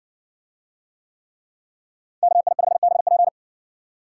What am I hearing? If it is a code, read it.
Morse code, heard as a keyed tone, about 60 wpm, 700 Hz. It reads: KI5BP